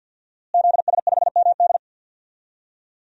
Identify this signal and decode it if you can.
Morse code, heard as a keyed tone, about 48 wpm, 700 Hz. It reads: ZS5MD
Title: DXCC-100 Call Signs - 5 Characters - Repeated 2x - 50wpm